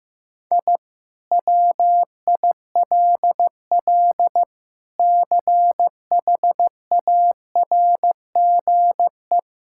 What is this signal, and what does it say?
Morse code, heard as a keyed tone, about 15 wpm, 700 Hz. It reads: I WILL CHARGE
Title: Sentences from Top 1000 Words - Review Part 6 - 15wpm